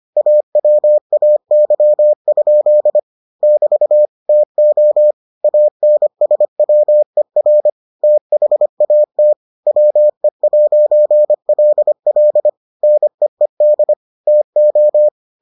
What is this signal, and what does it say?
Morse code, heard as a keyed tone, about 25 wpm, 600 Hz. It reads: AWAY? <BT> TO ANSWER THAT WE'LL NEED TO